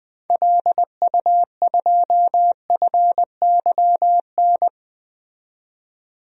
Morse code, heard as a keyed tone, about 20 wpm, 700 Hz. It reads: LU2FYN
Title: DXCC-100 Call Signs - 6 Characters - 20wpm